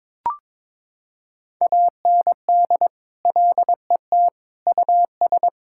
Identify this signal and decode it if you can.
Morse code, heard as a keyed tone, about 22 wpm, 700 Hz. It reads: AND LET US
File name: practice / 22wpm / Sets of 7 Words - 22wpm